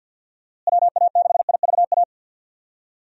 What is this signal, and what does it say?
Morse code, heard as a keyed tone, about 52 wpm, 700 Hz. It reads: WA6IVA
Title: USA Call Signs - 6 Characters - Repeated 3x - 50wpm